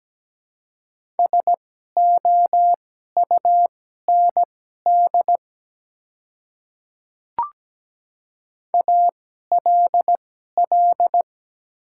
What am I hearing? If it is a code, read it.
Morse code, heard as a keyed tone, about 17 wpm, spaced as for 12 wpm, 700 Hz. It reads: SOUND ALL
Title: 2x Character Spacing - Top 100 Words - 17wpm